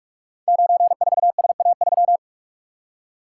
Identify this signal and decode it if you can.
Morse code, heard as a keyed tone, about 44 wpm, 690 Hz. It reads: 94SA3
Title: Code Groups - 5 Letter:Numbers - 45wpm